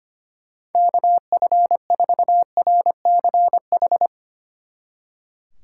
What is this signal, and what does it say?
Morse code, heard as a keyed tone, about 31 wpm, 700 Hz. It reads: KF4RCH
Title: Speed Racing - USA 2x3 Call Signs 25wpm